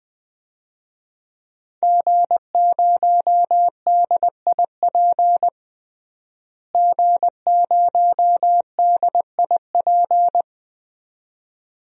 Morse code, heard as a keyed tone, about 20 wpm, 700 Hz. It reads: G0DIP G0DIP
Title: DXCC-100 Call Signs - 5 Characters - Repeated 2x - 20wpm